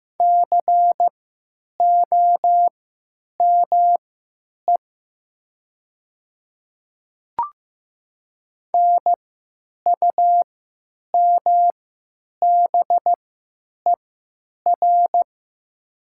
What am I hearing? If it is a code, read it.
Morse code, heard as a keyed tone, about 15 wpm, spaced as for 8 wpm, 700 Hz. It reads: COME NUMBER
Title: 3x Character Spacing - Top 100 Words - 15wpm